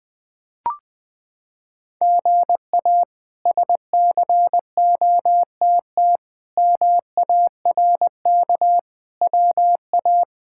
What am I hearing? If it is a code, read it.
Morse code, heard as a keyed tone, about 20 wpm, 700 Hz. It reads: GA SCOTT MARK WA
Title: K1USN SST QSO Practice - 20wpm